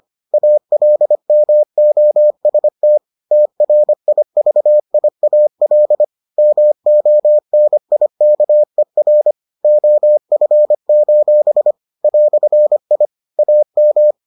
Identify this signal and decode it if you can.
Morse code, heard as a keyed tone, about 25 wpm, 600 Hz. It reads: ALMOST TRIVIAL MONIKER OF: "I AM